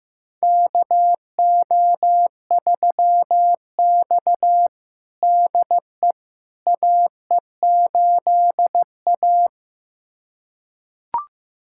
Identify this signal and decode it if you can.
Morse code, heard as a keyed tone, about 15 wpm, 700 Hz. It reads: KO3X DE AE8A
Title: USA Callsign DE Callsign - 4 Characters - 15wpm